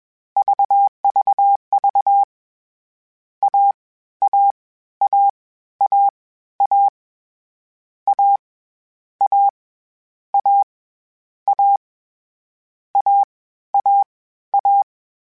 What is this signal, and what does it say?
Morse code, heard as a keyed tone, about 21 wpm, 800 Hz. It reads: VVV A A A A A A A A A A A A
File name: Difficult Combos / Difficult Combos-mp3s / AWJ1 @20